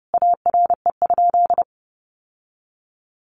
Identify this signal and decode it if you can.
Morse code, heard as a keyed tone, about 30 wpm, 700 Hz. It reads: ARE?
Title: Sentences from Top 1400 Words 30wpm